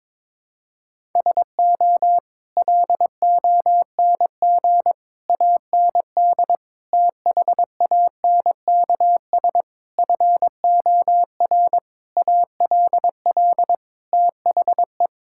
Morse code, heard as a keyed tone, about 22 wpm, 700 Hz. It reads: SO LONG AND THANKS FOR ALL THE
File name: practice / 22wpm / Sets of 8 Words - 22wpm